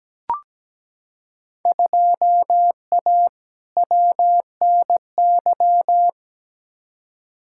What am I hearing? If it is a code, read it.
Morse code, heard as a keyed tone, about 17 wpm, 700 Hz. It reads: 2A WNY
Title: Field Day Exchanges - 17wpm